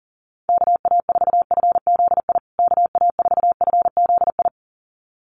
Morse code, heard as a keyed tone, about 40 wpm, 700 Hz. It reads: KA4FZI KA4FZI
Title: USA Call Signs Repeated 2x 40wpm